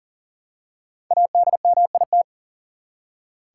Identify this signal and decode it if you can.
Morse code, heard as a keyed tone, about 40 wpm, 700 Hz. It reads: ADMIT
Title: Five Letter Words - 40wpm